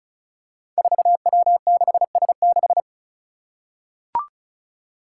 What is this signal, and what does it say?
Morse code, heard as a keyed tone, about 36 wpm, 700 Hz. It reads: 4W6S6
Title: Five Challenging Letter-Number Combinations 35wpm